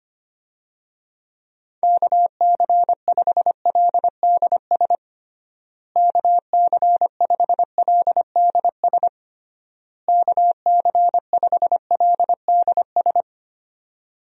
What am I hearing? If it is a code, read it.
Morse code, heard as a keyed tone, about 25 wpm, 700 Hz. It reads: KC5LDS KC5LDS KC5LDS
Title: USA Call Signs - 6 Characters - Repeated 3x - 25wpm